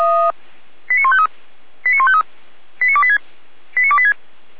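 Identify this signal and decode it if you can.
Five-tone selective call (ZVEI), heard as quick runs of five tones, tones of about 80 ms. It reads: 89153 89153 89286 89286
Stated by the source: ZVEI five-tone selective call received by SDR.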